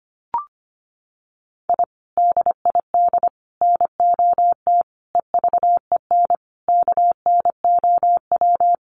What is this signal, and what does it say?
Morse code, heard as a keyed tone, about 25 wpm, 700 Hz. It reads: I DID NOT EVEN KNOW
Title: Sets of 5 Words Encore 25wpm